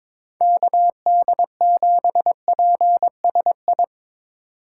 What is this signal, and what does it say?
Morse code, heard as a keyed tone, about 22 wpm, 700 Hz. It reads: KD7PSI
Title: USA Call Signs - 6 Characters - 6x Character Spacing - 22wpm